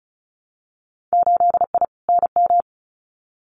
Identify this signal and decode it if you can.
Morse code, heard as a keyed tone, about 34 wpm, 700 Hz. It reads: 8I NM
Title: Winter Field Day Exchanges Encore 35wpm